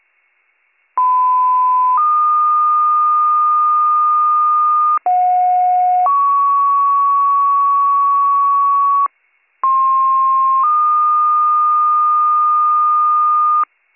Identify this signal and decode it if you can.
Two-tone page, heard as a one-second tone, then a three-second tone, about 1010 then 1220 Hz, about 730 then 1120 Hz, about 1060 then 1220 Hz. It is heard from 1.0 to 13.6 s.